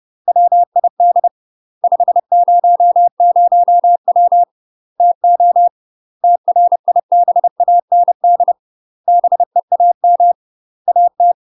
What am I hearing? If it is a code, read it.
Morse code, heard as a keyed tone, about 30 wpm, 700 Hz. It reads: WID 500W TO TRIBAND BEAM AT
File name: UK MASTER QSOs TRAINING / 30WPM / UK MASTER QSOs 0001